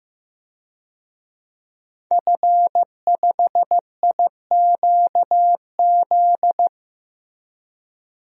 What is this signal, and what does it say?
Morse code, heard as a keyed tone, about 15 wpm, 700 Hz. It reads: F5IQZ
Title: DXCC-100 Call Signs - 5 Characters - Repeated 2x - 15wpm